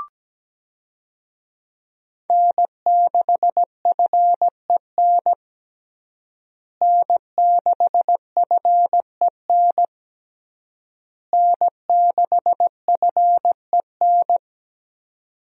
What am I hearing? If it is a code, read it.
Morse code, heard as a keyed tone, about 17 wpm, 700 Hz. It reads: N6FEN N6FEN N6FEN
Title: USA Call Signs - 5 Characters - Repeated 3x - 17wpm